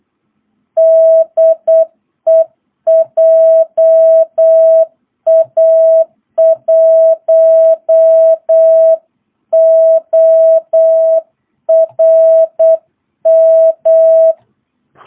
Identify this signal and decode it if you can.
Morse code, heard as a keyed tone, about 8 wpm, 650 Hz. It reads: DE JA1ORM
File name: jh8cbh / 2022cw